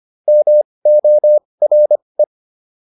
Morse code, heard as a keyed tone, about 25 wpm, spaced as for 20 wpm, 600 Hz. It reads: MORE